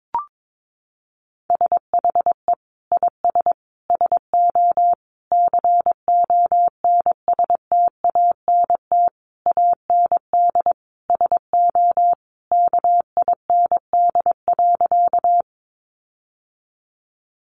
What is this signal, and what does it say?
Morse code, heard as a keyed tone, about 22 wpm, 700 Hz. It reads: SHE IS SO CONSTANT AND SO KIND.
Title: Sentences from Top 1500 Words 22wpm